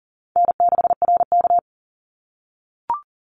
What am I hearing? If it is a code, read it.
Morse code, heard as a keyed tone, about 40 wpm, 700 Hz. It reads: N6RK
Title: Call Signs - 40wpm